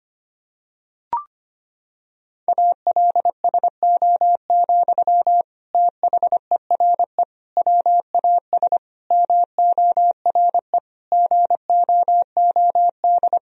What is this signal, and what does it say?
Morse code, heard as a keyed tone, about 25 wpm, 700 Hz. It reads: ALSO, THERE WAS MORE GOOD
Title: Sentences from Top 1000 Words - Review Part 1 - 25wpm